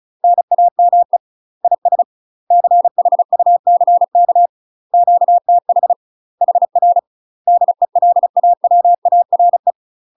Morse code, heard as a keyed tone, about 36 wpm, 700 Hz. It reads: NAME IS CHUCK QTH HR DELAWARE